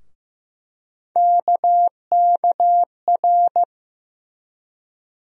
Morse code, heard as a keyed tone, about 16 wpm, 700 Hz. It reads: KKR
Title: USA Call Sign Postfixes - 3 Letters 15wpm